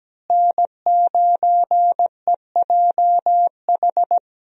Morse code, heard as a keyed tone, about 17 wpm, 700 Hz. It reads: N9EJH
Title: USA Call Signs - 5 Characters - Repeated 3x - 17wpm